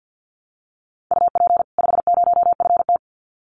Morse code, heard as a keyed tone, about 50 wpm, 700 Hz. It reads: UP 50FT
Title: Common QSO Phrases 50wpm